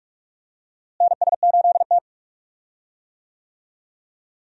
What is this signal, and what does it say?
Morse code, heard as a keyed tone, about 45 wpm, 700 Hz. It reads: NS8T
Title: USA Call Signs - 4 Characters - Repeated 2x - 45wpm